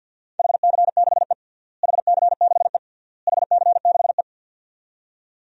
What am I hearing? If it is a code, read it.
Morse code, heard as a keyed tone, about 51 wpm, 700 Hz. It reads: HC6E HC6E HC6E
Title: DXCC-100 Call Signs - 4 Characters - Repeated 3x - 50wpm